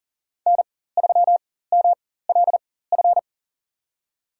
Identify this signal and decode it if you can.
Morse code, heard as a keyed tone, about 40 wpm, spaced as for 19 wpm, 700 Hz. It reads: N3MLF
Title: USA Call Signs - 5 Characters - 4x Character Spacing - 40wpm